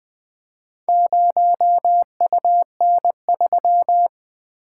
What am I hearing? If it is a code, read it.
Morse code, heard as a keyed tone, about 20 wpm, 700 Hz. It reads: 0UN3
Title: Four Letter-Number Combinations - 20wpm